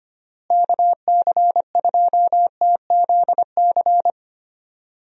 Morse code, heard as a keyed tone, about 25 wpm, 700 Hz. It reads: KC2TZC
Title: USA Call Signs - 6 Characters - Repeated 2x - 25wpm